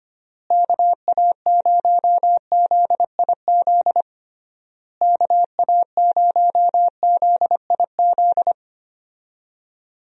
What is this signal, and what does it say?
Morse code, heard as a keyed tone, about 25 wpm, 700 Hz. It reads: KA0ZIZ KA0ZIZ
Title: USA Call Signs - 6 Characters - Repeated 3x - 25wpm